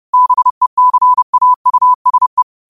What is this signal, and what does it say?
Morse code, heard as a keyed tone, about 30 wpm, 1000 Hz. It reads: BECAUSE